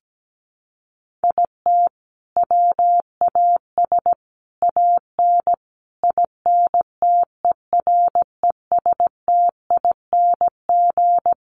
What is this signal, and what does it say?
Morse code, heard as a keyed tone, about 17 wpm, 700 Hz. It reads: IT WAS AN INTERESTING
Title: Sentences from Top 1100 Words 17wpm